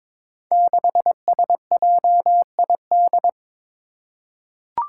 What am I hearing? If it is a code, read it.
Morse code, heard as a keyed tone, about 22 wpm, 700 Hz. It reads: 6SJID
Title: Five Challenging Letter-Number Combinations 22wpm